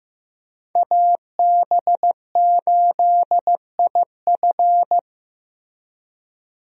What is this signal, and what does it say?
Morse code, heard as a keyed tone, about 15 wpm, 700 Hz. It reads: AB8IF
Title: USA Call Signs - 5 Characters - Repeated 2x - 15wpm